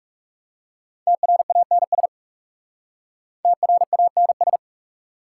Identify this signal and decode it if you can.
Morse code, heard as a keyed tone, about 37 wpm, 700 Hz. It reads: TRANS TRANS
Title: Speed-Racing Prefixes and Suffixes 30wpm